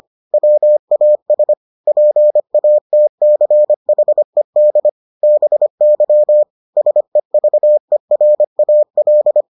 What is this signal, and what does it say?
Morse code, heard as a keyed tone, about 25 wpm, 600 Hz. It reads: WAS PATCHED BY SEVERAL